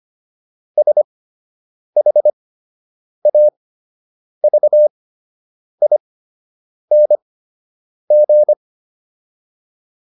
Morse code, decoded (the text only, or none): SHAVING